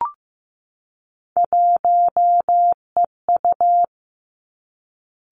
Morse code, heard as a keyed tone, about 15 wpm, 700 Hz. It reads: 1EU
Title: Three Challenging Letter-Number Combinations 15wpm